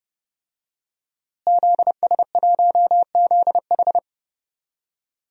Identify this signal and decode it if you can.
Morse code, heard as a keyed tone, about 30 wpm, 700 Hz. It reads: ZS1ZH
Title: DXCC-100 Call Signs - 5 Characters - Repeated 2x - 30wpm